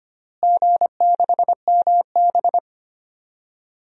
Morse code, heard as a keyed tone, about 25 wpm, 700 Hz. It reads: G6MB
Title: DXCC-100 Call Signs - 4 Characters - Repeated 2x - 25wpm